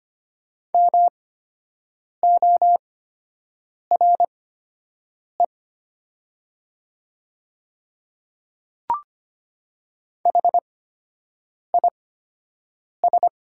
Morse code, heard as a keyed tone, about 25 wpm, spaced as for 7 wpm, 700 Hz. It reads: MORE HIS
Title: 8x Character Spacing - Top 100 Words - 25wpm